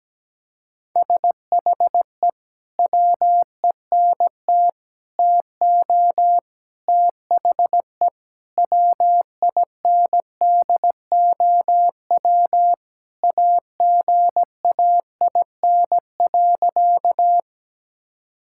Morse code, heard as a keyed tone, about 17 wpm, 700 Hz. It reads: SHE WENT TO THE WINDOW AGAIN.
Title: Sentences from Top 1600 Words 17wpm